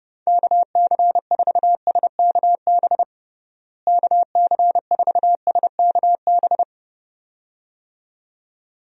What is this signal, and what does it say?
Morse code, heard as a keyed tone, about 30 wpm, 700 Hz. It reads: KC4SKB KC4SKB
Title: USA Call Signs - 6 Characters - Repeated 2x - 30wpm